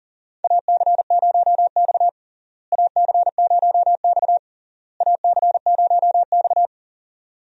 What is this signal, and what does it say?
Morse code, heard as a keyed tone, about 40 wpm, 700 Hz. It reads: AC0X AC0X AC0X
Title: USA Call Signs - 4 Characters - Repeated 3x - 40wpm